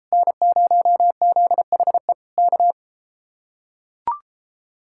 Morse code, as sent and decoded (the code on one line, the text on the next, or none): -. ----- --.. .... / -.-
N0ZHE K